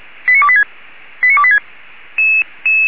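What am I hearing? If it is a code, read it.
Five-tone selective call (ZVEI), heard as quick runs of five tones, tones of about 70 ms. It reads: 89286 89286
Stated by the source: Five-tone ZVEI selective call, SDR receiver, 464 channel, feuerwehr-orlamuende fire brigade.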